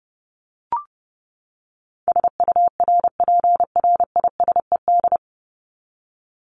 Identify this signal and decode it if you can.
Morse code, heard as a keyed tone, about 30 wpm, 700 Hz. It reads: SURPRISED
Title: Top 1501-1750 Words 30wpm